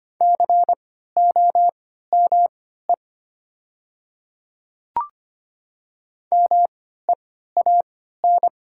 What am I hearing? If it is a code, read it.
Morse code, heard as a keyed tone, about 25 wpm, spaced as for 14 wpm, 690 Hz. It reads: COME MEAN
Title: 3x Character Spacing - Top 200 Words - 25wpm